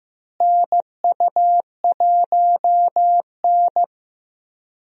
Morse code, heard as a keyed tone, about 15 wpm, 700 Hz. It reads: NU1N
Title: USA Call Signs - 4 Characters - Repeated 3x - 15wpm